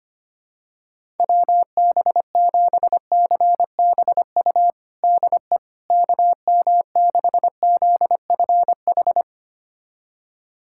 Morse code, heard as a keyed tone, about 25 wpm, 700 Hz. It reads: WB7CBU DE KM6ZFH
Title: USA Callsign DE Callsign - 6 Characters - 25wpm